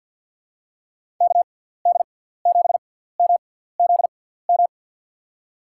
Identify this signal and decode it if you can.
Morse code, heard as a keyed tone, about 52 wpm, spaced as for 17 wpm, 700 Hz. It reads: KD7MZM
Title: USA Call Signs - 6 Characters - 6x Character Spacing - 50wpm